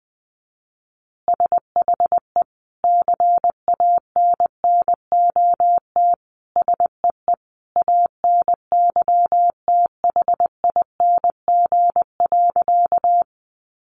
Morse code, heard as a keyed tone, about 20 wpm, 700 Hz. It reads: SHE CANNOT SEE ANYTHING.
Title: Additional Easy Phrases 20wpm